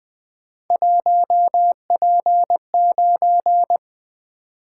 Morse code, heard as a keyed tone, about 20 wpm, 700 Hz. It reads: 1P9
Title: Three Letter-Number Combinations - 20wpm